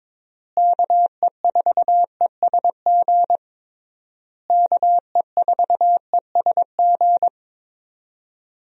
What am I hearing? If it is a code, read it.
Morse code, heard as a keyed tone, about 22 wpm, 700 Hz. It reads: KE4ESG KE4ESG
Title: USA Call Signs - 6 Characters - Repeated 3x - 22wpm